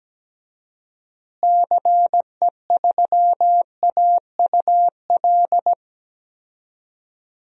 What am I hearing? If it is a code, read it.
Morse code, heard as a keyed tone, about 17 wpm, 700 Hz. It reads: CE3AUL
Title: DXCC-100 Call Signs - 6 Characters - Repeated 2x - 17wpm